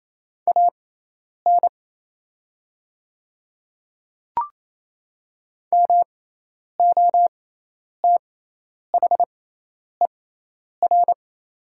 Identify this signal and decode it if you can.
Morse code, heard as a keyed tone, about 27 wpm, spaced as for 10 wpm, 700 Hz. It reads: AN MOTHER
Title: 6x Character Spacing - Top 200 Words - 28wpm